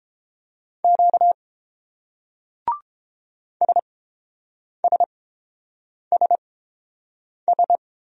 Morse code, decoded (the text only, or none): Q SSSS